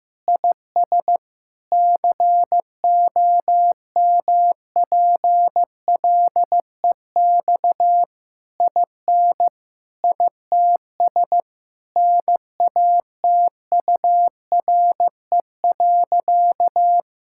Morse code, heard as a keyed tone, about 15 wpm, 700 Hz. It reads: IS COMPLEX IN ITS NATURE.
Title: Sentences from Top 1000 Words - Review Part 6 - 15wpm